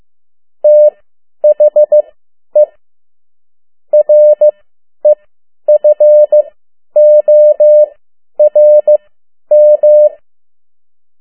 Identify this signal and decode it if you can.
Morse code, heard as a keyed tone, about 15 wpm, spaced as for 10 wpm, 600 Hz. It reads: THE REFORM
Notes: Morse code 10 WPM